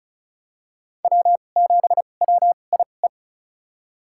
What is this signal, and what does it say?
Morse code, heard as a keyed tone, about 36 wpm, spaced as for 25 wpm, 700 Hz. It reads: W7WIE
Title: USA Call Signs - 5 Characters - 2x Character Spacing - 35wpm